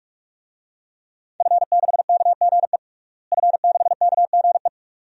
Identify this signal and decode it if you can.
Morse code, heard as a keyed tone, about 46 wpm, 700 Hz. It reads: F6KGE F6KGE
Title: DXCC-100 Call Signs - 5 Characters - Repeated 2x - 45wpm